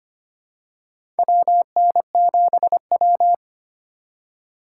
Morse code, heard as a keyed tone, about 25 wpm, 700 Hz. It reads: WN7W
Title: USA Call Signs - 4 Characters - 3x Character Spacing - 25wpm